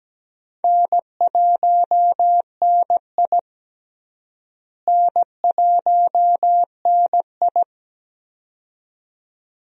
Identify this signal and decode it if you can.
Morse code, heard as a keyed tone, about 17 wpm, 700 Hz. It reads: N1NI N1NI
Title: USA Call Signs - 4 Characters - Repeated 2x - 17wpm